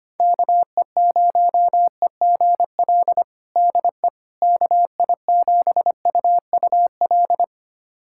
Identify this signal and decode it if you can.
Morse code, heard as a keyed tone, about 25 wpm, 700 Hz. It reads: KE0EGL DE KI7UUL
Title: USA Callsign DE Callsign 25wpm